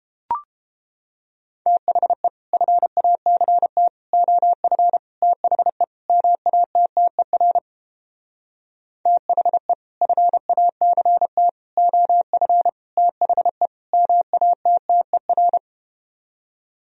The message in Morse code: THE FACT OF THE MATTER THE FACT OF THE MATTER

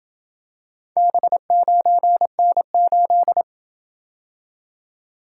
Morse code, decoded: B9N8

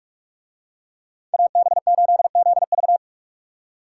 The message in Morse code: AB8ZV

